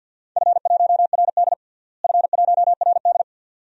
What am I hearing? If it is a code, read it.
Morse code, heard as a keyed tone, about 49 wpm, 700 Hz. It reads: F1RD F1RD
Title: DXCC-100 Call Signs - 4 Characters - Repeated 2x - 50wpm